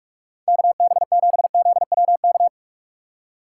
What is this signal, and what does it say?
Morse code, heard as a keyed tone, about 59 wpm, spaced as for 52 wpm, 700 Hz. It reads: KB7ZWK
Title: Speed Racing - USA 2x3 Call Signs 45wpm